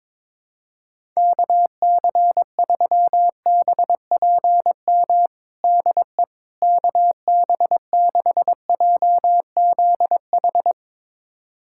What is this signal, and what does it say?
Morse code, heard as a keyed tone, about 22 wpm, 700 Hz. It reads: KC3BPM DE KB6JZH
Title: USA Callsign DE Callsign - 6 Characters - 22wpm